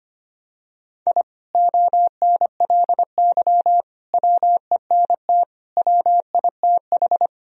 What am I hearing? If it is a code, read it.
Morse code, heard as a keyed tone, about 25 wpm, 700 Hz. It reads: I ONLY WENT WITH
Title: Sentences from Top 1000 Words - Review Part 1 - 25wpm